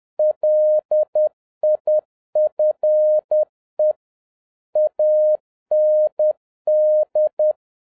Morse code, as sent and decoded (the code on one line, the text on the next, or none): .-.. .. ..-. / .- -. -..
LIFE AND